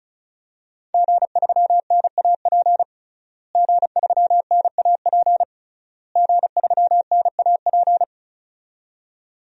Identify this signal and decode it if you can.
Morse code, heard as a keyed tone, about 36 wpm, 700 Hz. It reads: G3NAP G3NAP G3NAP
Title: DXCC-100 Call Signs - 5 Characters - Repeated 3x - 35wpm